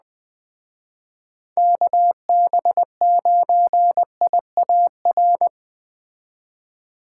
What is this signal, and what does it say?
Morse code, heard as a keyed tone, about 20 wpm, 700 Hz. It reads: KB9IAR
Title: Speed Racing - USA Call Signs 17wpm